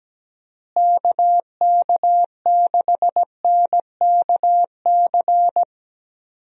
Morse code, heard as a keyed tone, about 17 wpm, 700 Hz. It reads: KK6NKC
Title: USA Call Signs - 6 Characters - Repeated 3x - 17wpm